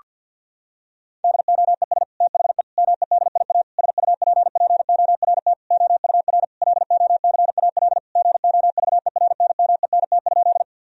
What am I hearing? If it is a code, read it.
Morse code, heard as a keyed tone, about 50 wpm, 700 Hz. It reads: DOES THE MEDIA SUPPORT OUR LOCAL GOVERNMENT?
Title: Sentences from Top 1000 Words - Review Part 4 - 50wpm